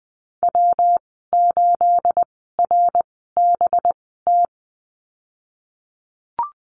Morse code, heard as a keyed tone, about 20 wpm, spaced as for 15 wpm, 700 Hz. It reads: W8RBT
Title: 219 Lesson 35g - Number 0 - Call Signs 20wpm